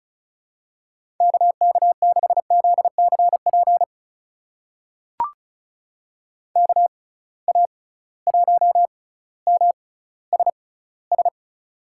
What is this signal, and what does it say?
Morse code, heard as a keyed tone, about 34 wpm, spaced as for 18 wpm, 700 Hz. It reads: KK6ZCP KA1MSS
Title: USA Call Signs - 6 Characters - 6x Character Spacing - 35wpm